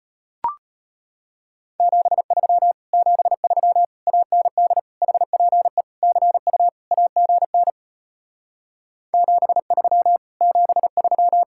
Morse code, heard as a keyed tone, about 33 wpm, 700 Hz. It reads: 73 73 AND HPE CU AGN 73 73